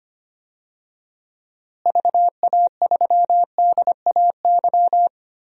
Morse code, heard as a keyed tone, about 25 wpm, 700 Hz. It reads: VA3DAY
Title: DXCC-100 Call Signs - 6 Characters - Repeated 3x - 25wpm